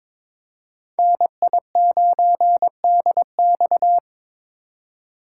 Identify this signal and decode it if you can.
Morse code, heard as a keyed tone, about 22 wpm, 700 Hz. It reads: NI9DX
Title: USA Call Signs - 5 Characters - 4x Character Spacing - 22wpm